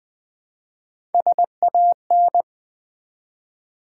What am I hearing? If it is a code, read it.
Morse code, heard as a keyed tone, about 20 wpm, 700 Hz. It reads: SAN
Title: Build A Word - 1 Letter At A Time - 3 to 5-Letter Words - 20wpm